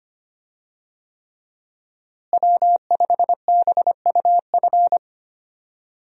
Morse code, heard as a keyed tone, about 25 wpm, 700 Hz. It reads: W5BUF
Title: USA Call Signs - 5 Characters - Repeated 3x - 25wpm